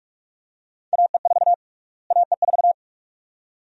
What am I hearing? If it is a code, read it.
Morse code, heard as a keyed tone, about 46 wpm, 700 Hz. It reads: AE4 AE4
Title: USA Call Sign Prefixes - Repeated 2x - 45wpm